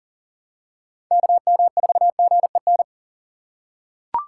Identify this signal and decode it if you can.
Morse code, heard as a keyed tone, about 40 wpm, 690 Hz. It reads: KM4GEN
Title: USA Call Signs - 6 Characters - 2x Character Spacing - 40wpm